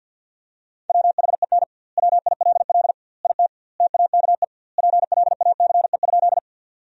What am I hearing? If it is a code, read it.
Morse code, heard as a keyed tone, about 50 wpm, 700 Hz. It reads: WHEN WILL IT TAKE PLACE?
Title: 2x Word-Spacing - Sentences from Top 200 Words - 50wpm